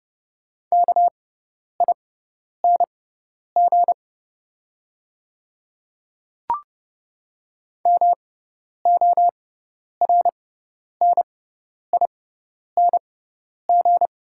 Morse code, heard as a keyed tone, about 30 wpm, spaced as for 10 wpm, 700 Hz. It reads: KING MORNING
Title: 6x Character Spacing - Top 500 Words - 30wpm